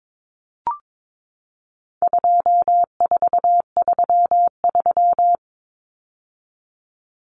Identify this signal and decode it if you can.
Morse code, heard as a keyed tone, about 22 wpm, 700 Hz. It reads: 2433
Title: Common Cut Numbers - 4-Digit 22wpm